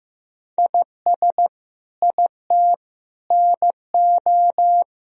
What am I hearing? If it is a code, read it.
Morse code, heard as a keyed tone, about 15 wpm, 700 Hz. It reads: IS IT NO
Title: Sentences from Top 1000 Words - Review Part 4 - 15wpm